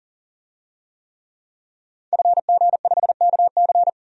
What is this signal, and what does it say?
Morse code, heard as a keyed tone, about 40 wpm, 700 Hz. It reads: FG5KC
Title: DXCC-100 Call Signs - 5 Characters - 40wpm